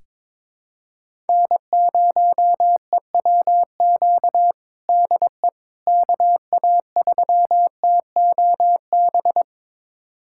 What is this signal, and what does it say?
Morse code, heard as a keyed tone, about 22 wpm, 700 Hz. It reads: N0EWQ DE KA3TOB
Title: USA Callsign DE Callsign 22wpm